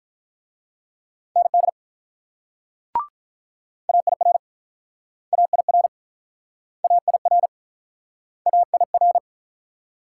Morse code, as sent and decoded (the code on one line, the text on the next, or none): -. -.. / .- .. .-. / .- .. .-. / .- .. .-. / .- .. .-.
ND AIR AIR AIR AIR